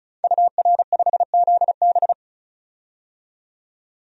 Morse code, heard as a keyed tone, about 36 wpm, 700 Hz. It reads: UR5ZB
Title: DXCC-100 Call Signs - 5 Characters - 35wpm